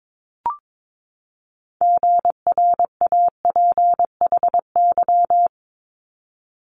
Morse code, heard as a keyed tone, about 22 wpm, 700 Hz. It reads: GRAPHY
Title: Multi-letter Phonemes 22wpm